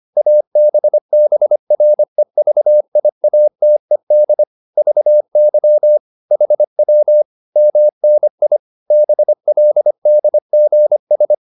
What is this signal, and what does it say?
Morse code, heard as a keyed tone, about 25 wpm, 600 Hz. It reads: ABBREVIATED VY HW MNI BLDGS